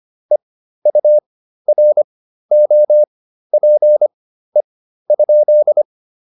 Morse code, heard as a keyed tone, about 25 wpm, spaced as for 13 wpm, 600 Hz. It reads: EUROPE?